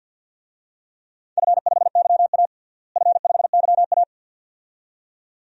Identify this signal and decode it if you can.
Morse code, heard as a keyed tone, about 51 wpm, 700 Hz. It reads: F5YA F5YA
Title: DXCC-100 Call Signs - 4 Characters - Repeated 2x - 50wpm